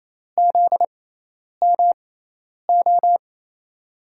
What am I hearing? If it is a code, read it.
Morse code, heard as a keyed tone, about 28 wpm, spaced as for 10 wpm, 700 Hz. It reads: ZMO
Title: USA Call Signs - 6 Characters - 6x Character Spacing - 28wpm